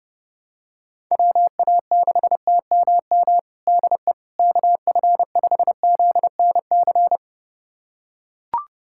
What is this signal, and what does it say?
Morse code, heard as a keyed tone, about 30 wpm, 700 Hz. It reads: WA6TMM DE KF5ZNC
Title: USA Callsign DE Callsign - 6 Characters - 30wpm